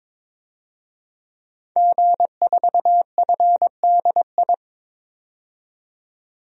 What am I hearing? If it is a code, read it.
Morse code, heard as a keyed tone, about 22 wpm, 700 Hz. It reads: G4FDI
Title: DXCC-100 Call Signs - 5 Characters - 22wpm